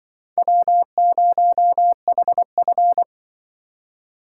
Morse code, heard as a keyed tone, about 25 wpm, 700 Hz. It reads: W0HF